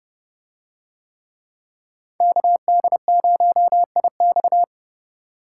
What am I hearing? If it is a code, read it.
Morse code, heard as a keyed tone, about 28 wpm, 700 Hz. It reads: KD0IX